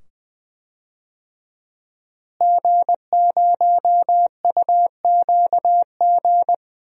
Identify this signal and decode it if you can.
Morse code, heard as a keyed tone, about 20 wpm, 700 Hz. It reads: G0UQG
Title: DXCC-100 Call Signs - 5 Characters - Repeated 2x - 20wpm